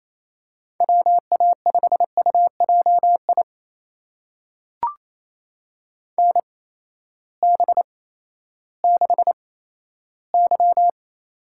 Morse code, heard as a keyed tone, about 27 wpm, spaced as for 14 wpm, 700 Hz. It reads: WA5UJI NB6Y
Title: USA Call Signs - 8x Character Spacing - 28wpm